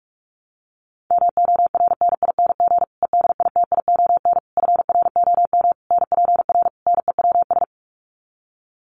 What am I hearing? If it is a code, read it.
Morse code, heard as a keyed tone, about 44 wpm, 700 Hz. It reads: MORNING EDITION FROM NPR NEWS
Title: Sets of 5 Words Encore 45wpm